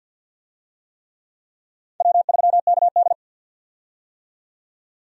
Morse code, heard as a keyed tone, about 52 wpm, 700 Hz. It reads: W3KD
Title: USA Call Signs - 4 Characters - 50wpm